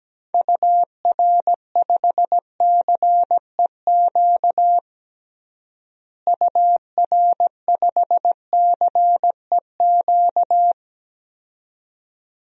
Morse code, heard as a keyed tone, about 17 wpm, 700 Hz. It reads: UR5CEQ UR5CEQ
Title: DXCC-100 Call Signs - 6 Characters - Repeated 2x - 17wpm